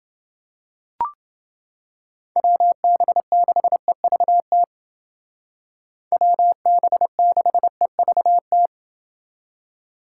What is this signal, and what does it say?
Morse code, heard as a keyed tone, about 29 wpm, 700 Hz. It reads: WB6EVT WB6EVT